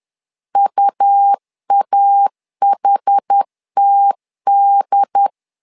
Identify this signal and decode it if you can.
Morse code, heard as a keyed tone, about 10 wpm, 800 Hz. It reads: UAHTD
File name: LEZIONI / 2 UHDN / 2 UHD 10wpm 1826